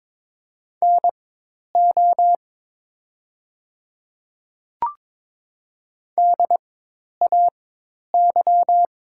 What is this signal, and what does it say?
Morse code, heard as a keyed tone, about 22 wpm, spaced as for 10 wpm, 690 Hz. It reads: NO DAY
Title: 4x Character Spacing - Top 200 Words - 22wpm